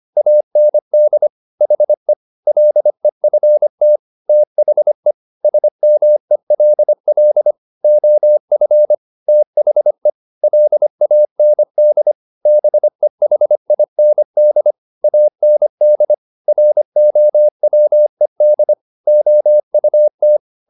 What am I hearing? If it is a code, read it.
Morse code, heard as a keyed tone, about 25 wpm, 600 Hz. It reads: AND HE LEFT THE SMELL OF THE LAND BEHIND AND ROWED OUT
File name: the old man and the sea part 1 25wpm